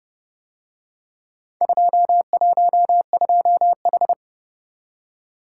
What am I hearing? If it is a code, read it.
Morse code, heard as a keyed tone, about 29 wpm, 700 Hz. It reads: 212H